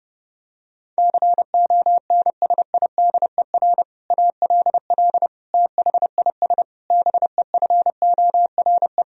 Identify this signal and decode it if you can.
Morse code, heard as a keyed tone, about 30 wpm, 700 Hz. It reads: CONSIDER ALL THIS BEFORE
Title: Sentences from Top 1000 Words - Review Part 4 - 30wpm